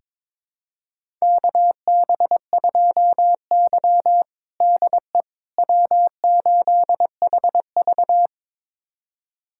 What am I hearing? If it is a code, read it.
Morse code, heard as a keyed tone, about 22 wpm, 700 Hz. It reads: KB2Y DE W8HV
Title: USA Callsign DE Callsign - 4 Characters - 22wpm